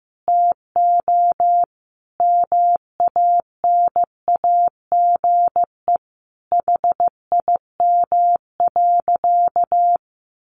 Morse code, heard as a keyed tone, about 15 wpm, 700 Hz. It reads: TO MANAGE HIM.
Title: Sentences from Top 1400 Words 15wpm